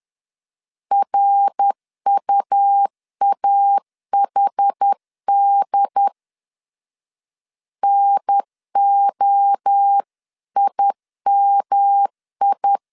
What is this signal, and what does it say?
Morse code, heard as a keyed tone, about 10 wpm, 800 Hz. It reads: RUAHD NOIMI
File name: CW / DET / 1 3 1500C S B 10wpm 3335